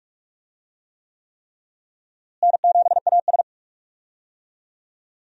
Morse code, heard as a keyed tone, about 41 wpm, 700 Hz. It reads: N7AS
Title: USA Call Signs - 4 Characters - Repeated 3x - 45wpm